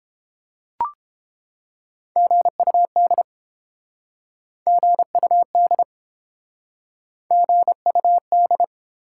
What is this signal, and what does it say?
Morse code, heard as a keyed tone, about 30 wpm, 700 Hz. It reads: GUD GUD GUD